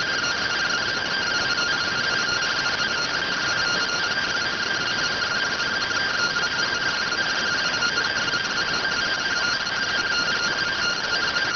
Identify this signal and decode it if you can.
Two-tone teleprinter signal, about 170 Hz shift.